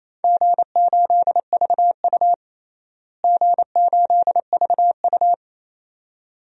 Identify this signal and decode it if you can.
Morse code, heard as a keyed tone, about 28 wpm, 700 Hz. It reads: G8VU G8VU
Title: DXCC-100 Call Signs - 4 Characters - Repeated 2x - 28wpm